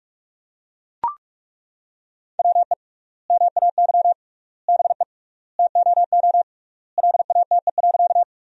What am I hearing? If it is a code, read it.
Morse code, heard as a keyed tone, about 41 wpm, 700 Hz. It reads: WE MAY BE TOO LATE.
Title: 3x Word-Spacing - Sentences from Top 300 Words - 45wpm